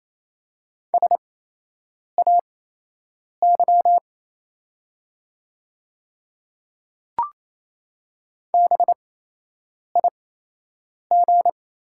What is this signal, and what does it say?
Morse code, heard as a keyed tone, about 28 wpm, spaced as for 8 wpm, 700 Hz. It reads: SAY BIG
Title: 8x Character Spacing - Top 200 Words - 28wpm